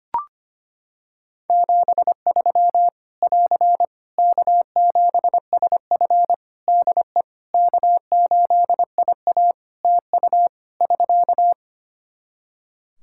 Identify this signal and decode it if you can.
Morse code, heard as a keyed tone, about 25 wpm, 700 Hz. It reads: 73 <AR> K7SF DE K8IA TU <SK>
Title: Practice QSO - Exchange 4 - Ending - 25wpm